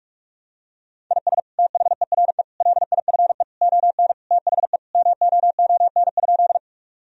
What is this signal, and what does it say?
Morse code, heard as a keyed tone, about 43 wpm, 700 Hz. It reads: IS THERE LIFE ON THE MOON?
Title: Sentences from Top 1000 Words - Review Part 3 - 45wpm